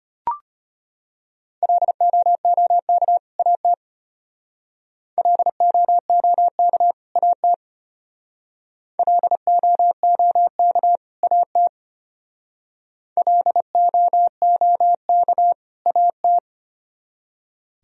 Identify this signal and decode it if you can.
Morse code, heard as a keyed tone, about 32 wpm, 700 Hz. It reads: LOOK AT LOOK AT LOOK AT LOOK AT